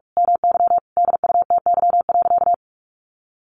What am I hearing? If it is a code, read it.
Morse code, heard as a keyed tone, about 44 wpm, 700 Hz. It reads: MY DUTY.
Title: Sentences from Top 1500 Words 45wpm